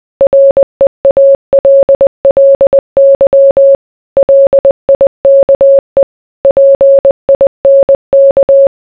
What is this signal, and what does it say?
Morse code, heard as a keyed tone, about 20 wpm, 550 Hz. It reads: REALLY LIKE PINK